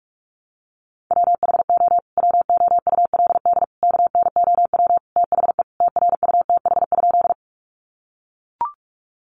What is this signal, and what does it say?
Morse code, heard as a keyed tone, about 44 wpm, 700 Hz. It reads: WHO WOULD KNOW THE TRUTH?
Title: Sentences from Top 800 Words 45wpm